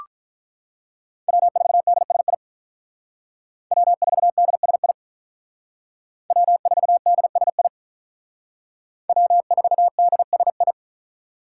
Morse code, heard as a keyed tone, about 45 wpm, 700 Hz. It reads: W4DSI W4DSI W4DSI W4DSI